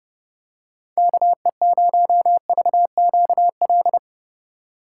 Morse code, heard as a keyed tone, about 30 wpm, 700 Hz. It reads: KE0VQL
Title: USA Call Signs - 6 Characters - 4x Character Spacing - 30wpm